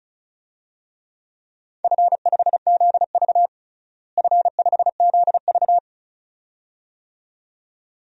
Morse code, heard as a keyed tone, about 36 wpm, 700 Hz. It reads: F5ZV F5ZV
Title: DXCC-100 Call Signs - 4 Characters - Repeated 2x - 35wpm